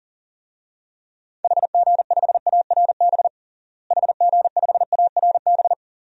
Morse code, heard as a keyed tone, about 40 wpm, 700 Hz. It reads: HG5ARB HG5ARB
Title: DXCC-100 Call Signs - 6 Characters - Repeated 2x - 40wpm